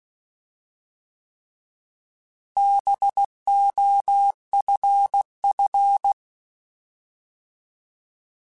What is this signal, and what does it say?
Morse code, heard as a keyed tone, about 16 wpm, 800 Hz. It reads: BOFF
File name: Alphabetical / Alphabetical mp3s / Bs @15